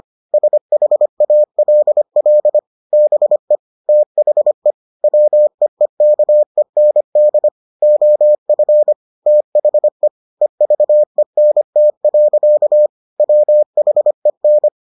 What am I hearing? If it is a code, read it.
Morse code, heard as a keyed tone, about 25 wpm, 600 Hz. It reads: SHALL BE THE WEEKEND OF THE EVENT. WHEN